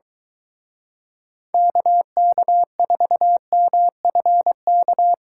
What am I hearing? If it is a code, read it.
Morse code, heard as a keyed tone, about 22 wpm, 700 Hz. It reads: KK4MFK